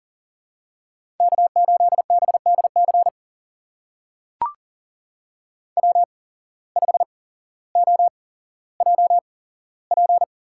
Morse code, heard as a keyed tone, about 41 wpm, spaced as for 17 wpm, 700 Hz. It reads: K8BDC W5OJP
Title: USA Call Signs - 8x Character Spacing - 40wpm